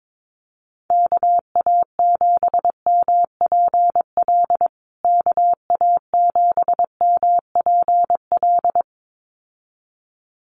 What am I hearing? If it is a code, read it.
Morse code, heard as a keyed tone, about 22 wpm, 700 Hz. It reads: KA7MPL KA7MPL
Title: USA Call Signs Repeated 2x 22wpm